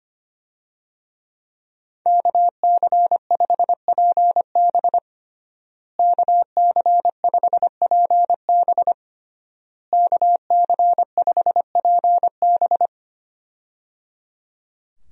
Morse code, heard as a keyed tone, about 25 wpm, 700 Hz. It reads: KC5PB KC5PB KC5PB
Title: USA Call Signs - 5 Characters - Repeated 3x - 25wpm